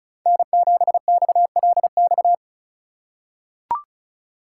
Morse code, heard as a keyed tone, about 34 wpm, 700 Hz. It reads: N7XLX
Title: USA Call Signs - 5 Characters - 4x Character Spacing - 35wpm